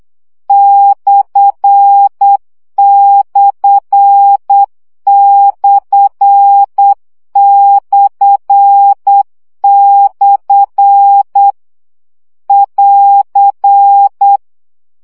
Morse code, heard as a keyed tone, about 8 wpm, 800 Hz. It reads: ///// <AR>
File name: CW gyakorlas per 7 baud